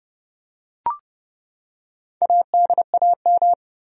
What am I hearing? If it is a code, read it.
Morse code, heard as a keyed tone, about 30 wpm, 700 Hz. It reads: ADAM
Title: USA Common Names - Max 5 Letters - 30wpm